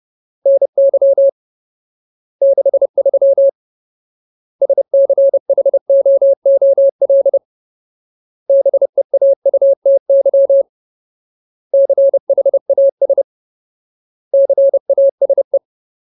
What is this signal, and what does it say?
Morse code, heard as a keyed tone, about 30 wpm, 550 Hz. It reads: NY 63 SCHOOL BEAUTY CHAS CASE
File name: Ultimate Word List / 30WPM Ultimate